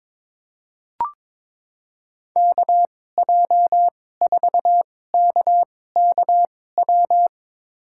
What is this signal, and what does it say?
Morse code, heard as a keyed tone, about 22 wpm, spaced as for 16 wpm, 700 Hz. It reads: KJ4KKW
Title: USA Call Signs - 2x Character Spacing - 22wpm